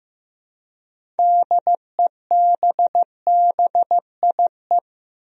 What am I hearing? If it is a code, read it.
Morse code, heard as a keyed tone, about 15 wpm, 700 Hz. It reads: DEBBIE
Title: USA Common Names - 15wpm